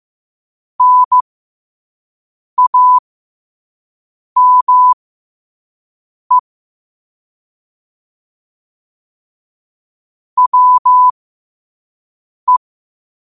Morse code, heard as a keyed tone, about 15 wpm, spaced as for 5 wpm, 1000 Hz. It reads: NAME WE
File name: qotdfiles / 5wpm